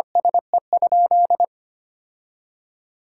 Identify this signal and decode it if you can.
Morse code, heard as a keyed tone, about 25 wpm, 700 Hz. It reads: SE?
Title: Sentences from Top 1000 Words - Review Part 5 - 25wpm